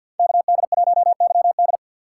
Morse code, heard as a keyed tone, about 49 wpm, 700 Hz. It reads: KD1YD
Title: USA Call Signs - 5 Characters - 2x Character Spacing - 50wpm